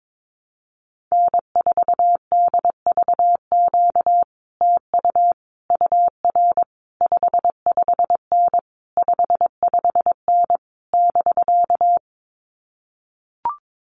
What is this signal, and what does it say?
Morse code, heard as a keyed tone, about 22 wpm, 700 Hz. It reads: N4DVQ TU UR 55N 55N <BK>
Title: Common QSO Phrases Encore 22wpm